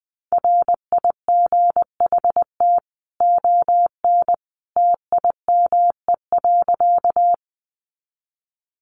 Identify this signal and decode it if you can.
Morse code, heard as a keyed tone, about 20 wpm, 700 Hz. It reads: RIGHT ON TIME.
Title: A Sound Of Thunder 20wpm